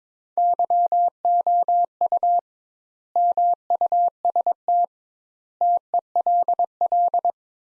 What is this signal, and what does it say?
Morse code, heard as a keyed tone, about 21 wpm, 700 Hz. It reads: YOU MUST TELL
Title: Sentences Top 200 Words - 22wpm